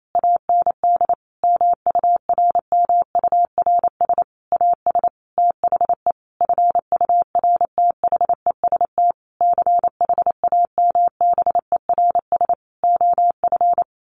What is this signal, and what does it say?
Morse code, heard as a keyed tone, about 28 wpm, 700 Hz. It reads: AND MURMURS AS THE FURTHEST CHAMBERS OF
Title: A Sound Of Thunder 28wpm